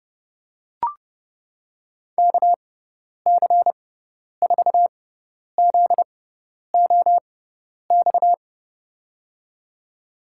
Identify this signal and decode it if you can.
Morse code, heard as a keyed tone, about 30 wpm, spaced as for 10 wpm, 700 Hz. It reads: KC4ZOX
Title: USA Call Signs - 6x Character Spacing - 30wpm